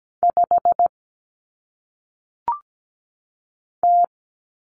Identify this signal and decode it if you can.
Morse code, heard as a keyed tone, about 17 wpm, 700 Hz. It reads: 5 T
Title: Common Cut Numbers - 1-Digit 17wpm